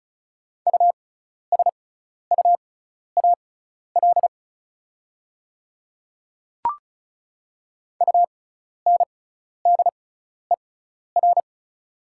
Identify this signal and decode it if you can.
Morse code, heard as a keyed tone, about 36 wpm, spaced as for 12 wpm, 700 Hz. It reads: USUAL UNDER
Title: 6x Character Spacing - Top 500 Words - 35wpm